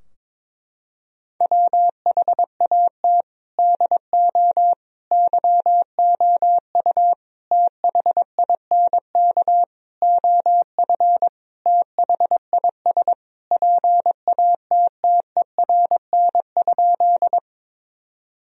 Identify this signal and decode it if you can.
Morse code, heard as a keyed tone, about 22 wpm, 700 Hz. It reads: WHAT DO YOU THINK OF THIS PATTERN?
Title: Sentences from Top 1000 Words - Review Part 2 - 22wpm